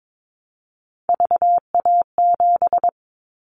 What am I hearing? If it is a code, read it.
Morse code, heard as a keyed tone, about 22 wpm, 700 Hz. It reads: VA7
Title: Callsign Prefixes for Top 50 DXCC - 22wpm